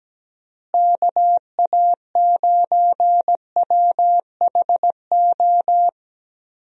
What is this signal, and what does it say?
Morse code, heard as a keyed tone, about 17 wpm, 700 Hz. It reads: KA9WHO
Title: USA Call Signs - 6 Characters - Repeated 2x - 17wpm